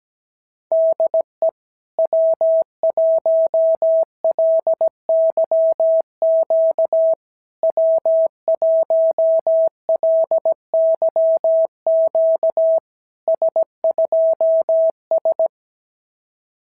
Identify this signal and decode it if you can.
Morse code, heard as a keyed tone, about 17 wpm, 650 Hz. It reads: DE W1LYQ W1LYQ S2S
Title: SOTA QSO Practice 17wpm